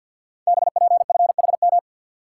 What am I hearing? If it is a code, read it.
Morse code, heard as a keyed tone, about 50 wpm, 700 Hz. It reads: BPFHM
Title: Code Groups - 5 Letters - 50wpm